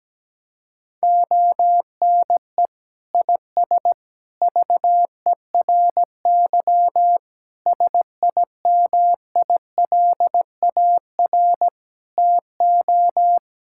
Morse code, heard as a keyed tone, about 17 wpm, 700 Hz. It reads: ONE IS VERY SIMILAR TO
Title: Sentences from Top 1000 Words - Review Part 5 - 17wpm